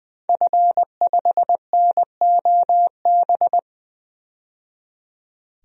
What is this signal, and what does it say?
Morse code, heard as a keyed tone, about 20 wpm, 700 Hz. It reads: F5NOB
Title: DXCC-100 Call Signs - 5 Characters - 20wpm